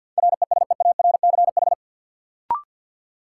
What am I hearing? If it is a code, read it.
Morse code, heard as a keyed tone, about 53 wpm, 700 Hz. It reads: RESEARCH
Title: Sentences from Top 1000 Words - Review Part 4 - New Word Drill - 50wpm